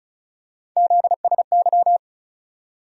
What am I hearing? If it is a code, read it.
Morse code, heard as a keyed tone, about 36 wpm, 700 Hz. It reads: ZSY
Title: USA Call Sign Postfixes - 3 Letters 35wpm